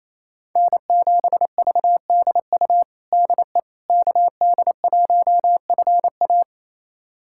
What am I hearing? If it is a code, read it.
Morse code, heard as a keyed tone, about 28 wpm, 700 Hz. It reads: N7VDU DE KD1FA
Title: USA Callsign DE Callsign - 5 Characters - 28wpm